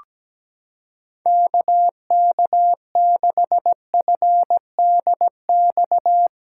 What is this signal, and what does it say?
Morse code, heard as a keyed tone, about 17 wpm, 700 Hz. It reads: KK6FDX
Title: USA Callsign DE Callsign - 6 Characters - 17wpm